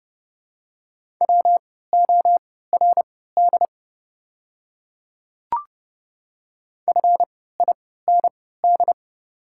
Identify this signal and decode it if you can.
Morse code, heard as a keyed tone, about 30 wpm, spaced as for 17 wpm, 700 Hz. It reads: WORD FIND
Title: 3x Character Spacing - Top 100 Words - 30wpm